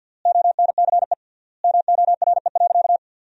Morse code, heard as a keyed tone, about 52 wpm, 700 Hz. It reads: ONCE MORE.
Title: 3x Word-Spacing - Sentences from Top 500 Words - 50wpm